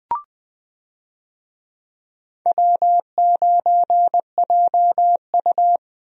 Morse code, heard as a keyed tone, about 20 wpm, 700 Hz. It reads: W9JU
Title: USA Call Signs - 4 Characters - 20wpm